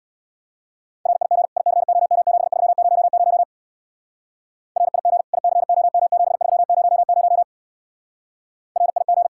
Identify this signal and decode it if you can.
Morse code, heard as a keyed tone, about 62 wpm, 700 Hz. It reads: RIG ICOM7300 RIG ICOM7300 RIG
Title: Speed-Racing Common QSO Phrases - 50wpm